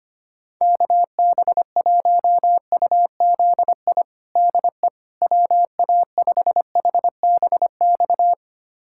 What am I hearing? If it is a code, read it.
Morse code, heard as a keyed tone, about 25 wpm, 700 Hz. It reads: KB1UZI DE WA5HBX
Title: USA Callsign DE Callsign 25wpm